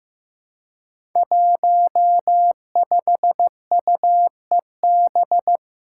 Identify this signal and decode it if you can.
Morse code, heard as a keyed tone, about 15 wpm, 700 Hz. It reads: 15UEB
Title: Five Challenging Letter-Number Combinations 15wpm